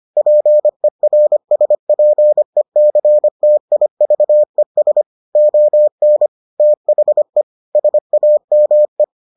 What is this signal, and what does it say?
Morse code, heard as a keyed tone, about 25 wpm, 600 Hz. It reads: PERSPECTIVES ON THE SAME